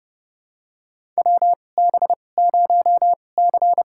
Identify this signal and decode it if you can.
Morse code, heard as a keyed tone, about 31 wpm, spaced as for 22 wpm, 700 Hz. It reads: WB0C
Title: USA Call Signs - 2x Character Spacing - 30wpm